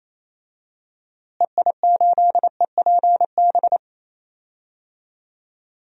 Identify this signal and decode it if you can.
Morse code, heard as a keyed tone, about 28 wpm, 700 Hz. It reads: EI8EPB
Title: DXCC-100 Call Signs - 6 Characters - Repeated 2x - 28wpm